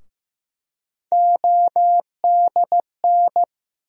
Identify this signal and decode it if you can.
Morse code, heard as a keyed tone, about 15 wpm, 700 Hz. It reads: ODN
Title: USA Call Sign Postfixes - 3 Letters 15wpm